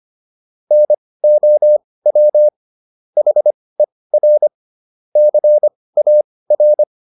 Morse code, heard as a keyed tone, about 25 wpm, spaced as for 18 wpm, 600 Hz. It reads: NOW HER CAR